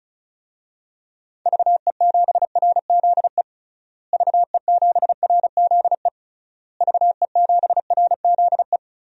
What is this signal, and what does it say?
Morse code, heard as a keyed tone, about 34 wpm, 700 Hz. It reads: VE7RZE VE7RZE VE7RZE
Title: DXCC-100 Call Signs - 6 Characters - Repeated 3x - 35wpm